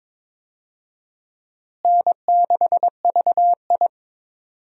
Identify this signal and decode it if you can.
Morse code, heard as a keyed tone, about 22 wpm, 700 Hz. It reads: N6VI
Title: USA Call Signs - 4 Characters - Repeated 2x - 22wpm